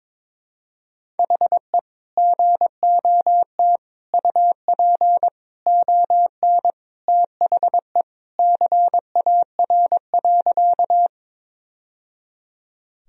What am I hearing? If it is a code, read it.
Morse code, heard as a keyed tone, about 22 wpm, 700 Hz. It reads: HE GOT UP ON THE CAR.
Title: Sentences from Top 1000 Words - Review Part 1 - 22wpm